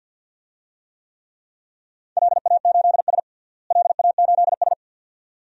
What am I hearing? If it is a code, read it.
Morse code, heard as a keyed tone, about 46 wpm, 700 Hz. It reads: LA8S LA8S
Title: DXCC-100 Call Signs - 4 Characters - Repeated 2x - 50wpm